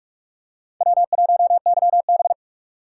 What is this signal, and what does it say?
Morse code, heard as a keyed tone, about 41 wpm, 700 Hz. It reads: W1YB
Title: USA Call Signs - 4 Characters - 2x Character Spacing - 45wpm